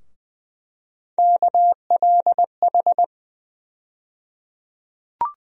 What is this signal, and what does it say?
Morse code, heard as a keyed tone, about 20 wpm, 700 Hz. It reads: KLH
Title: USA Call Sign Postfixes - 3 Letters 20wpm